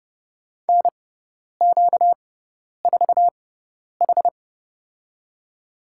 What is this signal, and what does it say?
Morse code, heard as a keyed tone, about 29 wpm, spaced as for 10 wpm, 700 Hz. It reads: NQ4H
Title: USA Call Signs - 4 Characters - 6x Character Spacing - 30wpm